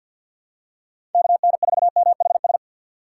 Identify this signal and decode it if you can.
Morse code, heard as a keyed tone, about 46 wpm, 700 Hz. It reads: KN4MHS
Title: USA Call Signs - 6 Characters - 6x Character Spacing - 50wpm